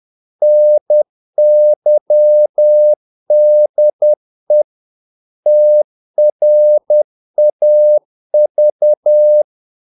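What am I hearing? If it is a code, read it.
Morse code, heard as a keyed tone, about 10 wpm, 600 Hz. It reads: NYDE TRAV
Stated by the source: Morse practice of short Danish words at 10 wpm, from CW med Gnister.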